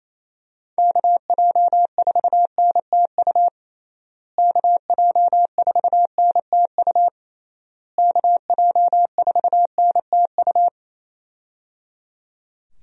Morse code, heard as a keyed tone, about 27 wpm, 700 Hz. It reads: KJ4NTU KJ4NTU KJ4NTU
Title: USA Call Signs - 6 Characters - Repeated 3x - 28wpm